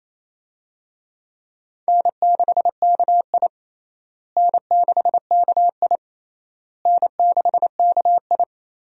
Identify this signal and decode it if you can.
Morse code, heard as a keyed tone, about 28 wpm, 700 Hz. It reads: N6KI N6KI N6KI
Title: USA Call Signs - 4 Characters - Repeated 3x - 28wpm